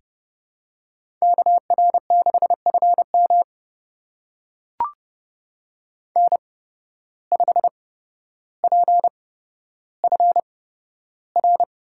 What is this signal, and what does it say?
Morse code, heard as a keyed tone, about 31 wpm, spaced as for 13 wpm, 700 Hz. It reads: KR6FM N5PFR
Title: USA Call Signs - 5 Characters - 8x Character Spacing - 30wpm